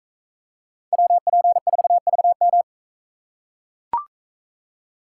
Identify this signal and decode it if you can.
Morse code, heard as a keyed tone, about 43 wpm, 700 Hz. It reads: WP4VM